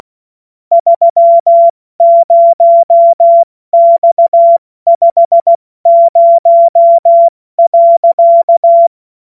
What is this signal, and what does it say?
Morse code, heard as a keyed tone, about 16 wpm, spaced as for 14 wpm, 680 Hz. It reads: 30X50.
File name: Morsekurs / Witze / Witz 20. Januar 2026 70 F80